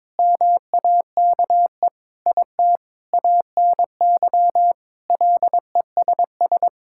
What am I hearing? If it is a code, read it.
Morse code, heard as a keyed tone, about 22 wpm, 700 Hz. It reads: MAKE IT ANY LESS
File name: Sentences with 7-Word Phrases - 22wpm